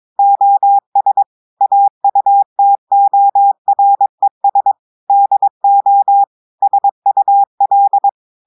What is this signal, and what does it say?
Morse code, heard as a keyed tone, about 22 wpm, 800 Hz. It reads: OS AUTORES DO SUL
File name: audios / portugais / portugais texte en clair 1 vitesse 22 mots minutes